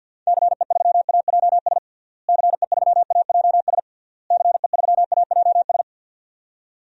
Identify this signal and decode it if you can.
Morse code, heard as a keyed tone, about 53 wpm, 700 Hz. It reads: CE3AJS CE3AJS CE3AJS
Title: DXCC-100 Call Signs - 6 Characters - Repeated 3x - 50wpm